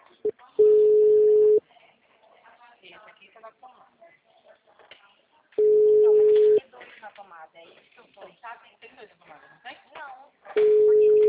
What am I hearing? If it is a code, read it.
Ringback tone (Europe, 425 Hz), with speech.
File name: out- -2016- - - .3530